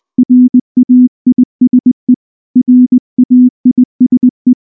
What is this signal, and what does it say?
Morse code, heard as a keyed tone, about 20 wpm, 250 Hz. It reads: RAISE RAISE
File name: morse / morse-training-20WPM-2